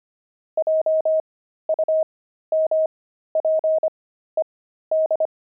Morse code, heard as a keyed tone, about 25 wpm, spaced as for 13 wpm, 630 Hz. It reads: JUMPED
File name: pangram-fs-13-cs-25-jumping